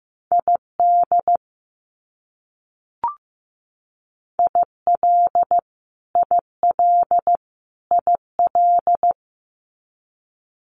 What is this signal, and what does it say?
Morse code, heard as a keyed tone, about 15 wpm, 700 Hz. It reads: ID IL IL IL
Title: US States Repeated 3x 15wpm